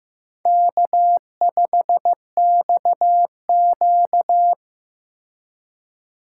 Morse code, heard as a keyed tone, about 15 wpm, 700 Hz. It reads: K5XQ
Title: K1USN SST QSO Practice - 15wpm